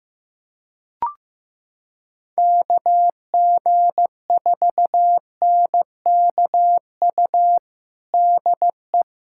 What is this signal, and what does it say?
Morse code, heard as a keyed tone, about 15 wpm, 700 Hz. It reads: KG4NKU DE
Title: USA Callsign DE Callsign - 6 Characters - 15wpm